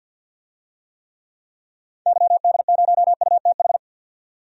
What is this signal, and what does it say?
Morse code, heard as a keyed tone, about 49 wpm, 700 Hz. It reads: YD0UTH
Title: DXCC-100 Call Signs - 6 Characters - 50wpm